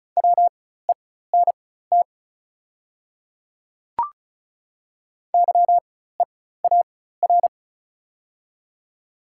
Morse code, heard as a keyed tone, about 33 wpm, spaced as for 16 wpm, 690 Hz. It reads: WENT YEAR
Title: 4x Character Spacing - Top 200 Words - 35wpm